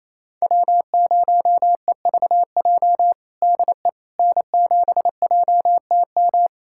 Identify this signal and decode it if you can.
Morse code, heard as a keyed tone, about 28 wpm, 700 Hz. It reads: W0EVJ DE N7JTM
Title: USA Callsign DE Callsign - 5 Characters - 28wpm